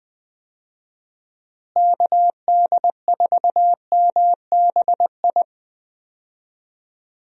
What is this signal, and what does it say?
Morse code, heard as a keyed tone, about 20 wpm, 700 Hz. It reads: KD4MBI
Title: USA Call Signs - 6 Characters - Repeated 2x - 20wpm